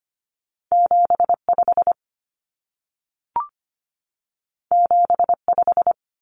Morse code, heard as a keyed tone, about 25 wpm, 700 Hz. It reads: 75 75
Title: Common Cut Numbers - 2-Digit 25wpm